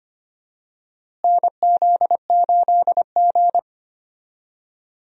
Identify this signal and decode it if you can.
Morse code, heard as a keyed tone, about 25 wpm, 700 Hz. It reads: NZ8G
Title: USA Call Signs - 4 Characters - Repeated 2x - 25wpm